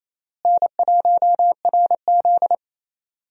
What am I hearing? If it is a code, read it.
Morse code, heard as a keyed tone, about 28 wpm, 700 Hz. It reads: N1RZ
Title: USA Call Signs - 4 Characters - Repeated 3x - 28wpm